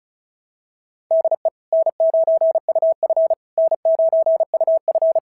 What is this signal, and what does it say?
Morse code, heard as a keyed tone, about 36 wpm, 690 Hz. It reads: DE N9UF N9UF
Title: SOTA QSO Practice 35wpm